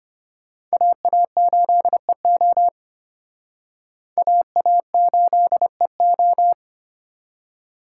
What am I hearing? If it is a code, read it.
Morse code, heard as a keyed tone, about 31 wpm, 700 Hz. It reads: AA8EO AA8EO